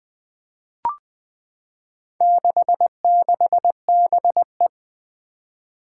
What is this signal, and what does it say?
Morse code, heard as a keyed tone, about 20 wpm, 700 Hz. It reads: 66BE
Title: Four Challenging Letter-Number Combinations 20wpm